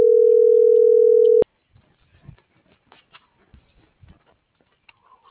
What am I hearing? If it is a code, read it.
Ringback tone (North America), 440 + 480 Hz.